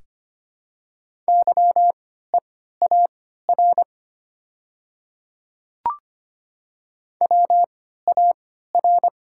Morse code, heard as a keyed tone, about 25 wpm, spaced as for 14 wpm, 700 Hz. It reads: YEAR WAR
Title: 3x Character Spacing - Top 500 Words - 25wpm